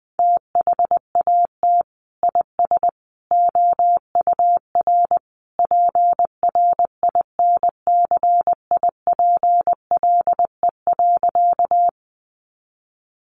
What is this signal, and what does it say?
Morse code, heard as a keyed tone, about 20 wpm, 700 Hz. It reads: THAT IS OUR PRINCIPLE.
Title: Sentences from Top 1400 Words 20wpm